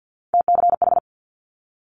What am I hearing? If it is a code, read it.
Morse code, heard as a keyed tone, about 50 wpm, 700 Hz. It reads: TCH
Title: Multi-letter Phonemes Quick Fire 50wpm